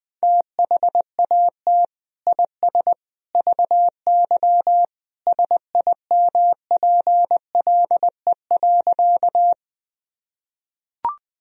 Morse code, heard as a keyed tone, about 21 wpm, 700 Hz. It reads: THAT IS VY SIMPLE.
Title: Sentences from Top 400 Words and Abbreviations 20wpm